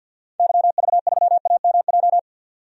Morse code, heard as a keyed tone, about 52 wpm, 700 Hz. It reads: YV3AMJ
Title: DXCC-100 Call Signs - 6 Characters - 50wpm